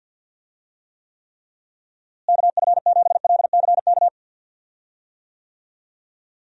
Morse code, heard as a keyed tone, about 51 wpm, 700 Hz. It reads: KF7LCK
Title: USA Call Signs - 6 Characters - Repeated 3x - 50wpm